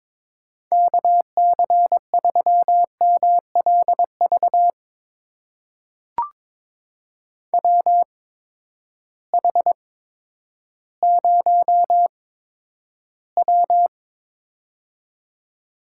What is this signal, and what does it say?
Morse code, heard as a keyed tone, about 22 wpm, spaced as for 11 wpm, 700 Hz. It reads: KC3MLV WH0W
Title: USA Call Signs - 8x Character Spacing - 22wpm